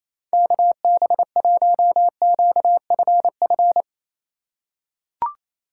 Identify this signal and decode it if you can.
Morse code, heard as a keyed tone, about 28 wpm, 700 Hz. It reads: KB1QFF
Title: USA Call Signs - 6 Characters - 4x Character Spacing - 28wpm